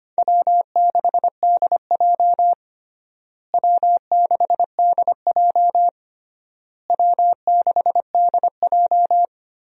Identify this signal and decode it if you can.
Morse code, heard as a keyed tone, about 25 wpm, 700 Hz. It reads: W6DJ W6DJ W6DJ
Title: USA Call Signs - 4 Characters - Repeated 3x - 25wpm